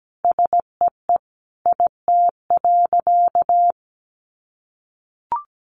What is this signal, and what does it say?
Morse code, heard as a keyed tone, about 17 wpm, 700 Hz. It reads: SEE IT.
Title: A Sound Of Thunder 17wpm